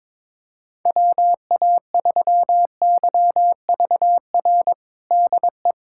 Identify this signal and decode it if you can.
Morse code, heard as a keyed tone, about 22 wpm, 700 Hz. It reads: WA3YVR DE
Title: USA Callsign DE Callsign 22wpm